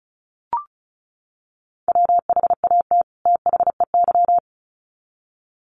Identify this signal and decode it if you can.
Morse code, heard as a keyed tone, about 34 wpm, 700 Hz. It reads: WHAT THEY
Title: Sets of 2 Words 35wpm